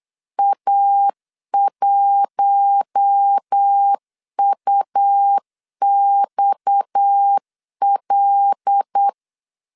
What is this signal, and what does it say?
Morse code, heard as a keyed tone, about 8 wpm, 800 Hz. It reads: A1UXL
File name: LEZIONI / 6 PKX0 / 6 PKX 8wpm 3041